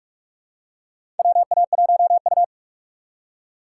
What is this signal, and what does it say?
Morse code, heard as a keyed tone, about 44 wpm, 690 Hz. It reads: WA1U
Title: USA Call Signs - 4 Characters - 3x Character Spacing - 45wpm